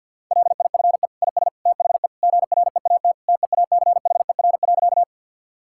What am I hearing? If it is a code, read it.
Morse code, heard as a keyed tone, about 50 wpm, 700 Hz. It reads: LIFE IS THE GREAT TEACHER.
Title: Sentences from Top 1000 Words - Review Part 6 - 50wpm